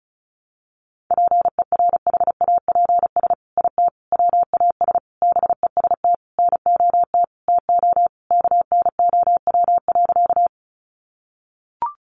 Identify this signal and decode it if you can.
Morse code, heard as a keyed tone, about 34 wpm, 700 Hz. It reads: PERHAPS IT WAS BEST NOT TO KNOW.
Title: Sentences from Top 600 Words 35wpm